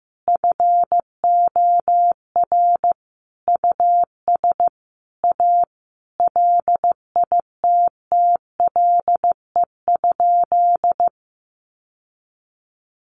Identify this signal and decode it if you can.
Morse code, heard as a keyed tone, about 15 wpm, 700 Hz. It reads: FOR US A LITTLE?
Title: Sentences from Top 900 Words 15wpm